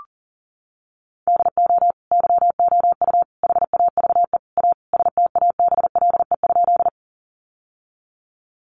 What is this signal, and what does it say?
Morse code, heard as a keyed tone, about 40 wpm, 700 Hz. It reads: DO YOU HAVE A STABLE?
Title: Sentences from Top 1500 Words 40wpm